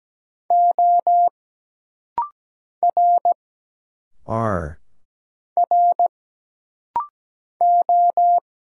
Morse code, heard as a keyed tone, about 17 wpm, 700 Hz. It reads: O R R O
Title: Single Letters - Rapid-Fire 17wpm